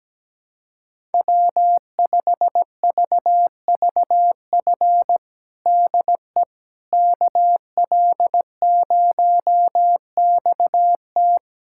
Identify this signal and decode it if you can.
Morse code, heard as a keyed tone, about 17 wpm, 700 Hz. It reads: W5VVF DE KL0XT
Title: USA Callsign DE Callsign - 5 Characters - 17wpm